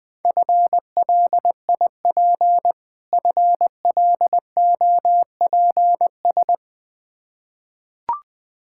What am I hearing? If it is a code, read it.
Morse code, heard as a keyed tone, about 21 wpm, 700 Hz. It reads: FLIP FLOPS
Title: Difficult Words with Hints - 20wpm